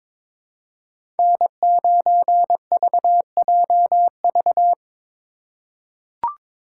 Morse code, heard as a keyed tone, about 22 wpm, 700 Hz. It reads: N9VJV
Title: USA Call Signs - 5 Characters - 2x Character Spacing - 22wpm